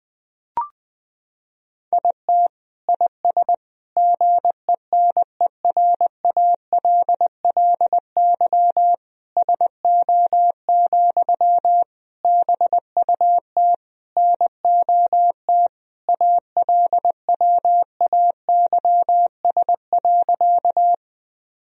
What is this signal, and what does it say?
Morse code, heard as a keyed tone, about 20 wpm, 700 Hz. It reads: IT IS GENERALLY SO, BUT NOT ALWAYS.
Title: Sentences from Top 1000 Words - Review Part 4 - 20wpm